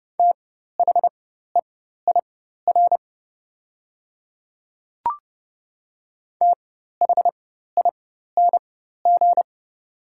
Morse code, heard as a keyed tone, about 30 wpm, spaced as for 14 wpm, 690 Hz. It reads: THEIR THING
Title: 4x Character Spacing - Top 100 Words - 30wpm